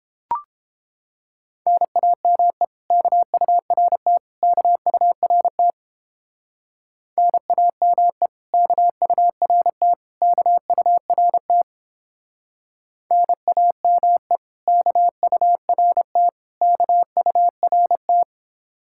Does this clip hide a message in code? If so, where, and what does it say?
Yes: Morse code from 1.7 to 18.2 s, heard as a keyed tone, about 31 wpm, 700 Hz. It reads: NAME KURT KURT NAME KURT KURT NAME KURT KURT